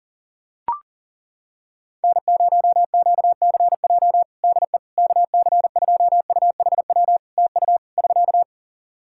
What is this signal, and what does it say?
Morse code, heard as a keyed tone, about 37 wpm, 700 Hz. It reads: N0QCJ DE KC2UHW TU <SK>
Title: K1USN SST QSO Practice 40wpm